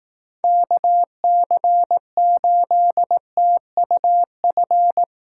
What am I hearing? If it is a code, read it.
Morse code, heard as a keyed tone, about 18 wpm, 700 Hz. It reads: KC8TUF
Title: Speed Racing - USA 2x3 Call Signs 15wpm